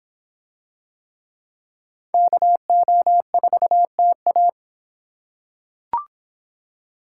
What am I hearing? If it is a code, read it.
Morse code, heard as a keyed tone, about 25 wpm, 700 Hz. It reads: KO4TA